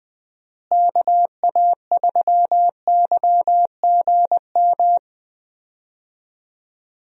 Morse code, heard as a keyed tone, about 21 wpm, 700 Hz. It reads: KA3YGM
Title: USA Call Signs - 6 Characters - Repeated 3x - 20wpm